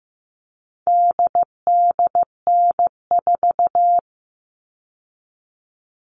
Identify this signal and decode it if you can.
Morse code, heard as a keyed tone, about 15 wpm, 700 Hz. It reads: DDN4
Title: All Cut Numbers - 4-Digit 15wpm